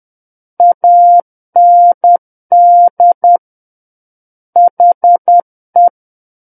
Morse code, heard as a keyed tone, about 10 wpm, 700 Hz. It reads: AND HE
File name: Book One / 01-Chapter 1 - 10 WPM